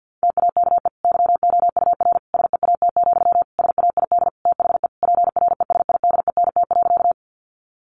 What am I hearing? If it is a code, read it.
Morse code, heard as a keyed tone, about 50 wpm, 700 Hz. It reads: TAKE YOUR SEAT, SAID THE PRESIDENT.
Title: Sentences from Top 1500 Words 50wpm